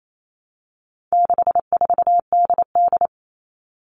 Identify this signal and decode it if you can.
Morse code, heard as a keyed tone, about 28 wpm, 700 Hz. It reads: 64DD
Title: All Cut Numbers - 4-Digit 28wpm